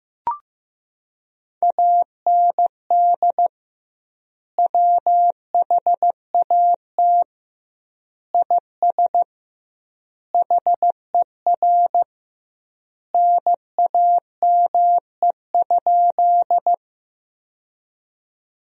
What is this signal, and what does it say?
Morse code, heard as a keyed tone, about 15 wpm, 700 Hz. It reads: AND WHAT IS HER NAME?
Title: 2x Word-Spacing - Sentences from Top 200 Words - 15wpm